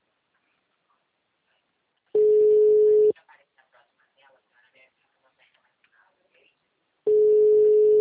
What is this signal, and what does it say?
Ringback tone (Europe, 425 Hz), with speech.